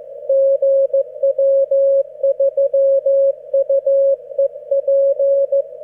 Morse code, heard as a keyed tone, about 15 wpm, 550 Hz. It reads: GW3UEP